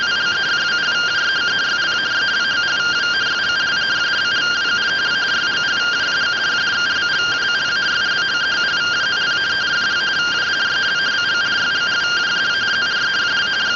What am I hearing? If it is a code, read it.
Two-tone teleprinter signal, about 170 Hz shift.